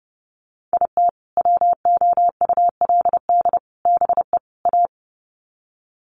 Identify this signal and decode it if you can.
Morse code, heard as a keyed tone, about 30 wpm, 700 Hz. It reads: IT WOULD BE A
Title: Sets of 4 Words 30wpm